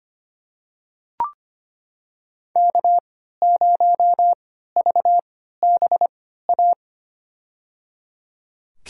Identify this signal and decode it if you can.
Morse code, heard as a keyed tone, about 25 wpm, spaced as for 14 wpm, 700 Hz. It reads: K0VBA
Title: USA Call Signs - 5 Characters - 3x Character Spacing - 25wpm